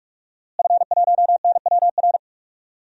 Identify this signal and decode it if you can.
Morse code, heard as a keyed tone, about 44 wpm, 700 Hz. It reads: F1NWR
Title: DXCC-100 Call Signs - 5 Characters - Repeated 2x - 45wpm